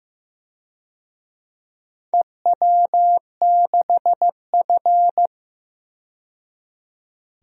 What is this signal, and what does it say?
Morse code, heard as a keyed tone, about 15 wpm, 700 Hz. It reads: EW6F
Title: DXCC-100 Call Signs - 4 Characters - 15wpm